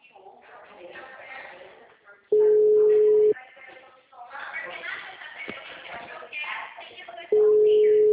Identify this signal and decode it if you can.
Ringback tone (Europe, 425 Hz), with speech.